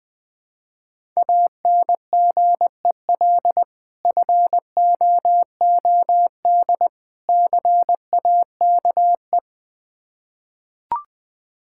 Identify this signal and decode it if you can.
Morse code, heard as a keyed tone, about 20 wpm, 700 Hz. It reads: ANGEL FOOD CAKE
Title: Difficult Words with Hints - 20wpm